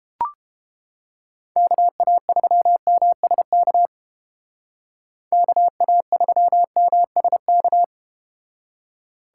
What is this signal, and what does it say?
Morse code, heard as a keyed tone, about 30 wpm, 700 Hz. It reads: KA3MSK KA3MSK